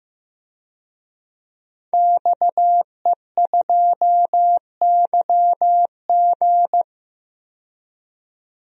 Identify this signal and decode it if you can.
Morse code, heard as a keyed tone, about 15 wpm, 700 Hz. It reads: XE2YG
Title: DXCC-100 Call Signs - 5 Characters - 15wpm